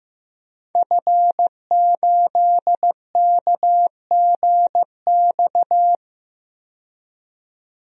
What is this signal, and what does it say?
Morse code, heard as a keyed tone, about 15 wpm, 700 Hz. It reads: F8KGX
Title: DXCC-100 Call Signs - 5 Characters - 15wpm